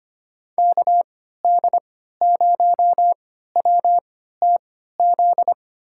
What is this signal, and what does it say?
Morse code, heard as a keyed tone, about 25 wpm, spaced as for 14 wpm, 700 Hz. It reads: KD0WTZ
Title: USA Call Signs - 6 Characters - 3x Character Spacing - 25wpm